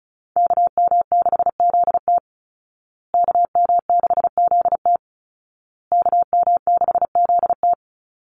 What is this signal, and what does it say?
Morse code, heard as a keyed tone, about 34 wpm, 700 Hz. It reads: KM6ZT KM6ZT KM6ZT
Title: Call Signs Repeated 2x - 35wpm